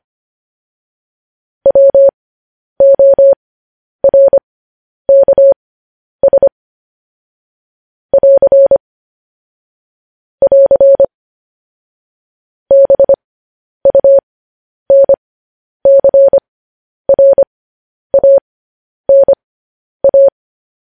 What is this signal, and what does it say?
Morse code, heard as a keyed tone, about 25 wpm, spaced as for 10 wpm, 550 Hz. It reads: WORKS <AR> <AR> BUNCRANA
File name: podcast / News headlines-1x-25x10